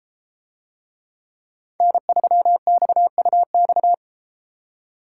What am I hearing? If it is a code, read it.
Morse code, heard as a keyed tone, about 32 wpm, 700 Hz. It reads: N3XUX